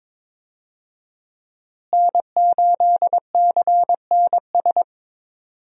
Morse code, heard as a keyed tone, about 22 wpm, 700 Hz. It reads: N8CNS
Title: USA Call Signs - 5 Characters - Repeated 3x - 22wpm